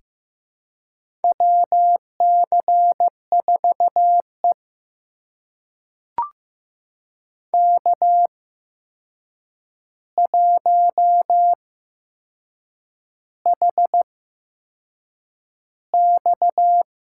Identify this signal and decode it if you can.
Morse code, heard as a keyed tone, about 15 wpm, spaced as for 6 wpm, 700 Hz. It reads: WC4E K1HX
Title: USA Call Signs - 4 Characters - 8x Character Spacing - 15wpm